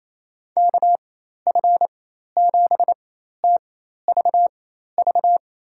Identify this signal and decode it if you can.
Morse code, heard as a keyed tone, about 27 wpm, spaced as for 13 wpm, 700 Hz. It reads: KF7TVV
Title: USA Call Signs - 6 Characters - 4x Character Spacing - 28wpm